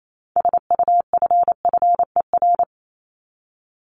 Morse code, heard as a keyed tone, about 28 wpm, 700 Hz. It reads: SUFFER
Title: ING Suffix - Top 100 Words with Base Words 28wpm